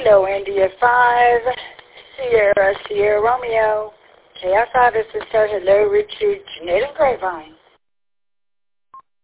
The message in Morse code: none

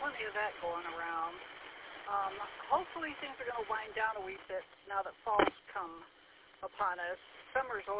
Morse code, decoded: none

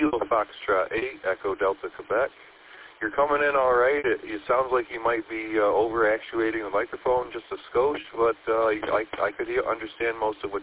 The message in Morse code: none

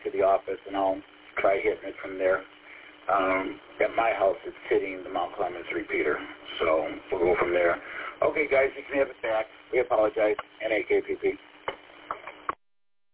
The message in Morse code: none

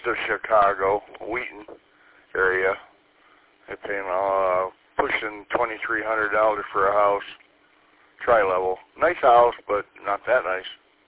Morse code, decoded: none

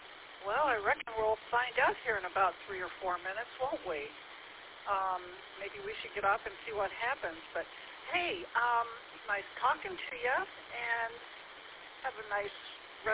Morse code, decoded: none

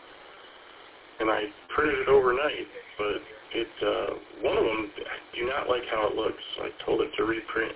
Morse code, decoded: none

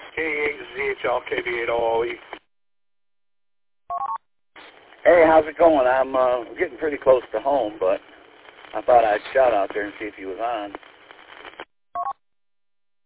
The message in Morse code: none

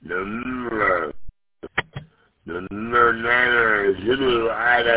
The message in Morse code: none

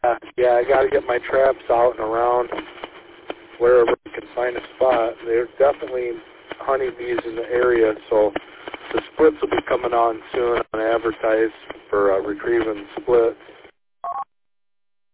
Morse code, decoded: none

